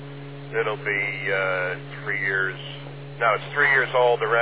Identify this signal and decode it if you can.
Emergency Alert System (SAME) tones, heard as harsh buzzing bursts, with speech. It is heard from 0.8 to 3.9 s.